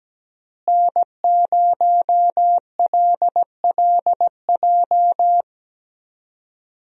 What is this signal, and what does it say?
Morse code, heard as a keyed tone, about 17 wpm, 700 Hz. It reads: N0LLJ
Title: USA Call Signs - 5 Characters - Repeated 2x - 17wpm